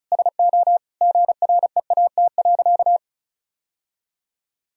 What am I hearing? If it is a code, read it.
Morse code, heard as a keyed tone, about 36 wpm, 700 Hz. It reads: SO GREAT.
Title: Sentences from Top 1000 Words - Review Part 4 - 35wpm